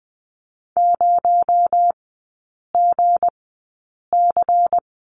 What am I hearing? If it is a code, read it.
Morse code, heard as a keyed tone, about 20 wpm, spaced as for 8 wpm, 700 Hz. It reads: 0GC
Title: 218 Lesson 35e - Number 0 - Three Characters 20wpm